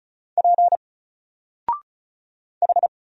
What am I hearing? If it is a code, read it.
Morse code, heard as a keyed tone, about 34 wpm, 700 Hz. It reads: P H
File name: Single-Letter-Number-35wpm